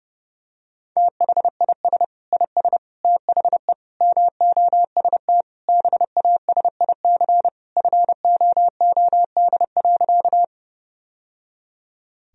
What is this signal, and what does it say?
Morse code, heard as a keyed tone, about 30 wpm, 700 Hz. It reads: THIS IS THE MOST BASIC FOOD.
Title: Sentences from Top 1000 Words - Review Part 3 - 30wpm